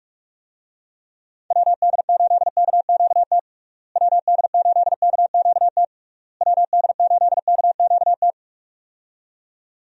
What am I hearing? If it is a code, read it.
Morse code, heard as a keyed tone, about 46 wpm, 700 Hz. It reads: WD8KQT WD8KQT WD8KQT
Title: USA Call Signs - 6 Characters - Repeated 3x - 45wpm